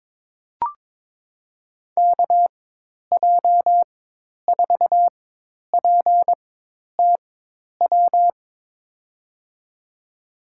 Morse code, heard as a keyed tone, about 22 wpm, spaced as for 10 wpm, 700 Hz. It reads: KJ4PTW
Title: USA Call Signs - 6 Characters - 4x Character Spacing - 22wpm